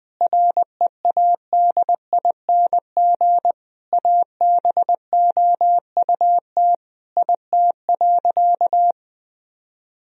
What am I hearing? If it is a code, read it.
Morse code, heard as a keyed tone, about 20 wpm, 700 Hz. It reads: READING ABOUT IT.
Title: Sentences from Top 1000 Words - Review Part 3 - 20wpm